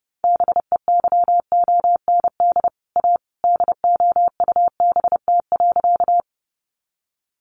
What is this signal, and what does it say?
Morse code, heard as a keyed tone, about 30 wpm, 700 Hz. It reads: BEYOND A DOUBT.
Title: Sentences from Top 1500 Words 30wpm